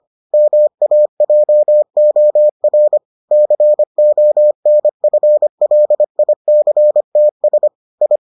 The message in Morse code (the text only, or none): MAJOR CONFLICTS I